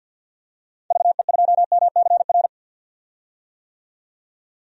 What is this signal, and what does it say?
Morse code, heard as a keyed tone, about 48 wpm, 700 Hz. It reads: VE2MCR
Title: DXCC-100 Call Signs - 6 Characters - 50wpm